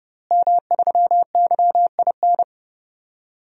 Morse code, heard as a keyed tone, about 31 wpm, 700 Hz. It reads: M3YIN
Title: DXCC-100 Call Signs - 5 Characters - Repeated 2x - 30wpm